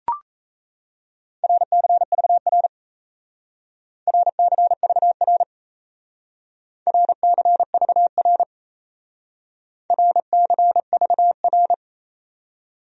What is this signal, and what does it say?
Morse code, heard as a keyed tone, about 36 wpm, 700 Hz. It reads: RCVR RCVR RCVR RCVR